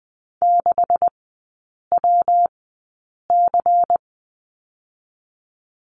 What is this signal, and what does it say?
Morse code, heard as a keyed tone, about 20 wpm, spaced as for 8 wpm, 700 Hz. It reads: 6WC
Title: 138 Lesson 23e - Letter f - Three Characters 20wpm